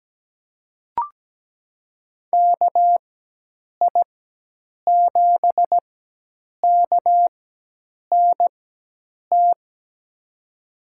Morse code, heard as a keyed tone, about 17 wpm, spaced as for 8 wpm, 700 Hz. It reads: KI7KNT
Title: USA Call Signs - 6 Characters - 4x Character Spacing - 17wpm